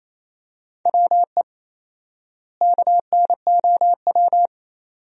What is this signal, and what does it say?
Morse code, heard as a keyed tone, about 28 wpm, 700 Hz. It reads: WE KNOW
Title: 4x Word-Spacing - Sentences from Top 500 Words - 28wpm